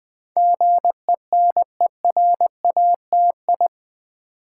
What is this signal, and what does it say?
Morse code, heard as a keyed tone, about 20 wpm, 700 Hz. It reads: GENERATI
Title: Build A Word - 4 Letters At A Time - 10-Letter Words - 20wpm